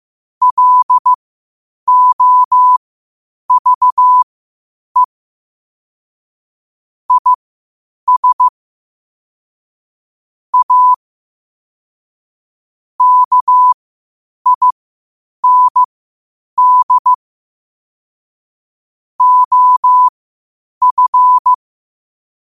Morse code, heard as a keyed tone, about 15 wpm, spaced as for 8 wpm, 1000 Hz. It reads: LOVE IS A KIND OF
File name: qotdfiles / 8wpm